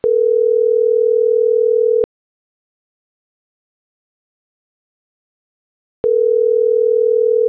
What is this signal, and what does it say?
Ringback tone (North America), 440 + 480 Hz.